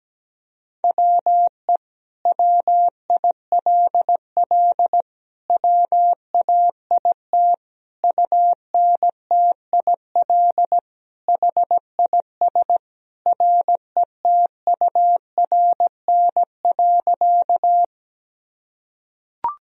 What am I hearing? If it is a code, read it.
Morse code, heard as a keyed tone, about 17 wpm, 700 Hz. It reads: WE WILL WAIT UNTIL HIS RETURN.
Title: Sentences from Top 1000 Words - Review Part 1 - 17wpm